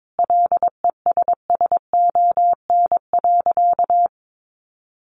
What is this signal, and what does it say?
Morse code, heard as a keyed tone, about 22 wpm, 700 Hz. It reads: LESSON.
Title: Sentences from Top 1200 Words 22wpm